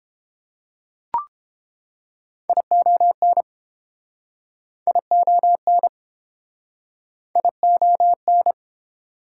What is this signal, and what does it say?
Morse code, heard as a keyed tone, about 31 wpm, 700 Hz. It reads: ION ION ION